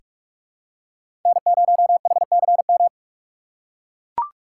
Morse code, heard as a keyed tone, about 44 wpm, 700 Hz. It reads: N0HCM
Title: USA Call Signs - 5 Characters - 8x Character Spacing - 45wpm